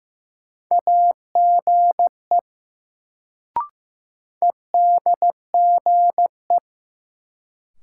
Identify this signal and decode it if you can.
Morse code, heard as a keyed tone, about 15 wpm, 700 Hz. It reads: AGE EDGE
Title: Sets of 3 Words - New Word Drill - 15wpm